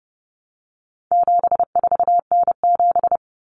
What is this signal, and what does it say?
Morse code, heard as a keyed tone, about 30 wpm, 700 Hz. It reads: 74N7
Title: Common Cut Numbers - 4-Digit 30wpm